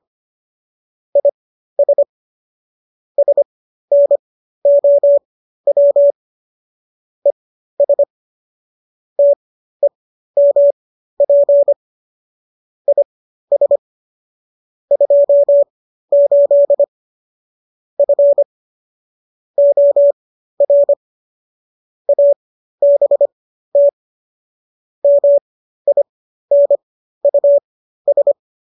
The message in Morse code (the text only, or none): IS SNOW ES TEMP IS 28 F OR ABT MINUS